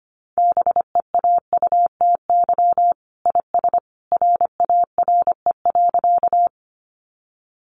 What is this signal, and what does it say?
Morse code, heard as a keyed tone, about 25 wpm, 700 Hz. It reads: BEAUTY IS RARE.
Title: Sentences from Top 1100 Words 25wpm